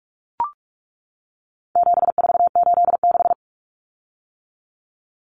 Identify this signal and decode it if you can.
Morse code, heard as a keyed tone, about 44 wpm, 700 Hz. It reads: 7486
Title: Common Cut Numbers - 4-Digit 45wpm